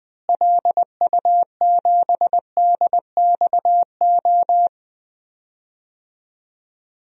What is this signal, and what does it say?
Morse code, heard as a keyed tone, about 20 wpm, 700 Hz. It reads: LU7DXO
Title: DXCC-100 Call Signs - 6 Characters - Repeated 2x - 20wpm